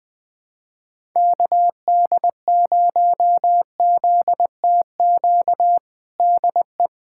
Morse code, heard as a keyed tone, about 20 wpm, 700 Hz. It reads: KD0ZTQ DE
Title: USA Callsign DE Callsign 20wpm